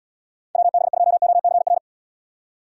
Morse code, heard as a keyed tone, about 75 wpm, 700 Hz. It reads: WB2OCR